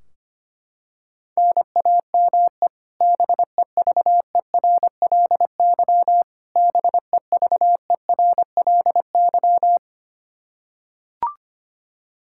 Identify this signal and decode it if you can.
Morse code, heard as a keyed tone, about 25 wpm, 700 Hz. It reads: NAME BEVERLY BEVERLY
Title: Practice QSO - Exchange 1 - Beginning - 25wpm